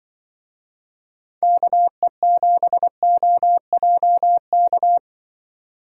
Morse code, heard as a keyed tone, about 25 wpm, 700 Hz. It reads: KE7OJK
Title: Speed Racing - USA 2x3 Call Signs 20wpm